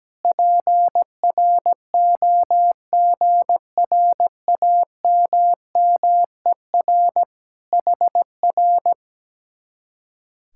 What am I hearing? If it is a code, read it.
Morse code, heard as a keyed tone, about 17 wpm, 700 Hz. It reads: PROGRAMMER HR
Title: Speed-Racing Common QSO Phrases - 17wpm